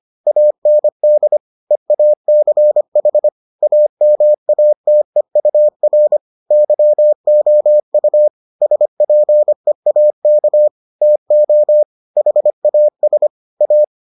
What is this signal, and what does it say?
Morse code, heard as a keyed tone, about 25 wpm, 600 Hz. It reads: AND EACH AMATEUR YOU SPEAK TO HAS A